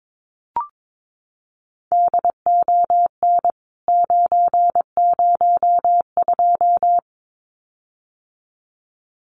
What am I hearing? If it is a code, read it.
Morse code, heard as a keyed tone, about 22 wpm, 700 Hz. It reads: DON 902
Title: CWT Exchange 22wpm